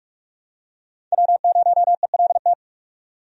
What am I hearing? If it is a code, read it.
Morse code, heard as a keyed tone, about 42 wpm, 700 Hz. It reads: W0ELT
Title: USA Call Signs - 5 Characters - 45wpm